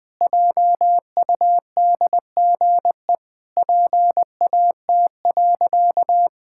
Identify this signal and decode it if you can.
Morse code, heard as a keyed tone, about 19 wpm, 700 Hz. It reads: JUDGE PAT.
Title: Pangram Challenge v2 - 20wpm